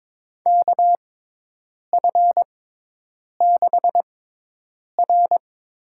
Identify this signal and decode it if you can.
Morse code, heard as a keyed tone, about 22 wpm, spaced as for 8 wpm, 700 Hz. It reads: KF6R
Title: USA Call Signs - 6 Characters - 6x Character Spacing - 22wpm